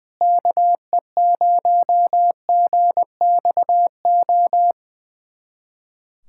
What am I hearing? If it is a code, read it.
Morse code, heard as a keyed tone, about 20 wpm, 700 Hz. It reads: KE0GXO
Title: USA Callsign DE Callsign - 6 Characters - 20wpm